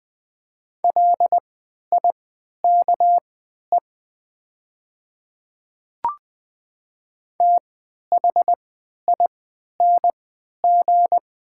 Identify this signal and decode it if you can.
Morse code, heard as a keyed tone, about 20 wpm, spaced as for 11 wpm, 700 Hz. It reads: LIKE THING
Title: 3x Character Spacing - Top 100 Words - 20wpm